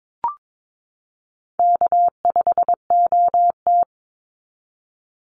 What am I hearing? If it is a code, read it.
Morse code, heard as a keyed tone, about 22 wpm, 700 Hz. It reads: K5OT
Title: Call Signs Repeated 2x - 22wpm